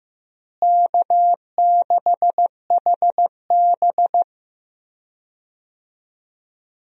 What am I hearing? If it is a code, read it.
Morse code, heard as a keyed tone, about 16 wpm, 700 Hz. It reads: K6HB
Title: USA Call Signs - 4 Characters - Repeated 2x - 15wpm